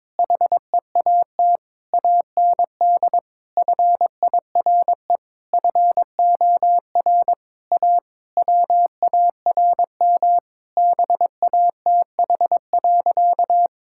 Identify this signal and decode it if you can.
Morse code, heard as a keyed tone, about 21 wpm, 700 Hz. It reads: HEAT AND FIRE FOR A WARM BATH.
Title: Sentences from Top 1600 Words 22wpm